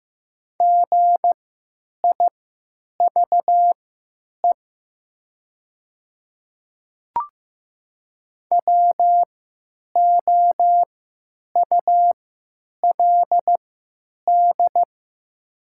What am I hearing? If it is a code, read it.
Morse code, heard as a keyed tone, about 15 wpm, spaced as for 9 wpm, 700 Hz. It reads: GIVE WOULD
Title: 3x Character Spacing - Top 200 Words - 15wpm